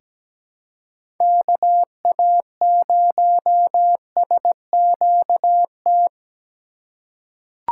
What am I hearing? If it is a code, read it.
Morse code, heard as a keyed tone, about 17 wpm, 700 Hz. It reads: KA0SQT
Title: USA Call Signs - 6 Characters - 2x Character Spacing - 17wpm